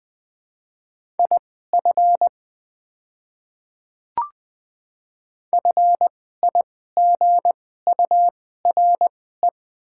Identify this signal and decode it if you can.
Morse code, heard as a keyed tone, about 20 wpm, spaced as for 15 wpm, 700 Hz. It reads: IF FIGURE
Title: 2x Character Spacing - Top 500 Words - 20wpm